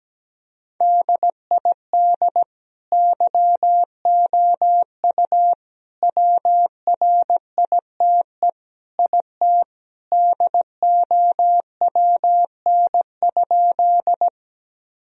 Morse code, heard as a keyed tone, about 17 wpm, 700 Hz. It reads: DID YOU WRITE IT DOWN?
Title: Sentences from Top 1000 Words - Review Part 1 - 17wpm